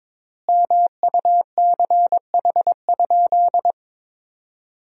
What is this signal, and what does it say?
Morse code, heard as a keyed tone, about 22 wpm, 700 Hz. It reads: MUCH?
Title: 4x Word-Spacing - Sentences from Top 200 Words - 22wpm